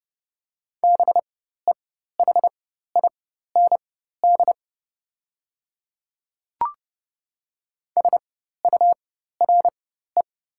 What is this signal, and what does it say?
Morse code, heard as a keyed tone, about 30 wpm, spaced as for 14 wpm, 700 Hz. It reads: BEHIND SURE
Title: 4x Character Spacing - Top 500 Words - 30wpm